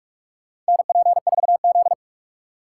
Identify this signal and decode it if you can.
Morse code, heard as a keyed tone, about 44 wpm, 700 Hz. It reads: NP4Z
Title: USA Call Signs - 4 Characters - 4x Character Spacing - 45wpm